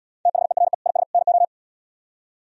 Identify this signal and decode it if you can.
Morse code, heard as a keyed tone, about 76 wpm, 700 Hz. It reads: THERE IS NO